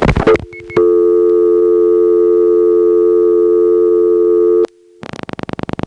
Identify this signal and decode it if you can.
Dial tone (North America), 350 + 440 Hz.
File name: phonetrips / SmyrnaGA5HQ